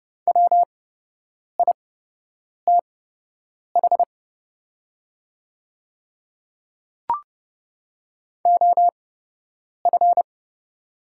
Morse code, heard as a keyed tone, about 29 wpm, spaced as for 8 wpm, 690 Hz. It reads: WITH OF
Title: 8x Character Spacing - Top 200 Words - 30wpm